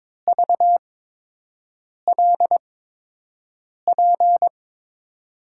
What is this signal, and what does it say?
Morse code, heard as a keyed tone, about 22 wpm, spaced as for 6 wpm, 700 Hz. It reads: VLP
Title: USA Call Signs - 6 Characters - 8x Character Spacing - 22wpm